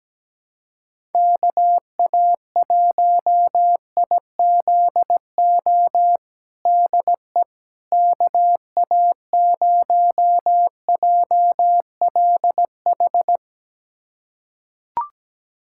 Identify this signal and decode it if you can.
Morse code, heard as a keyed tone, about 17 wpm, 700 Hz. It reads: KA1IZO DE KA0JLH
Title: USA Callsign DE Callsign - 6 Characters - 17wpm